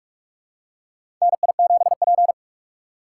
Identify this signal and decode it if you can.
Morse code, heard as a keyed tone, about 43 wpm, 700 Hz. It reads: NI7P
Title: USA Call Signs - 4 Characters - 2x Character Spacing - 45wpm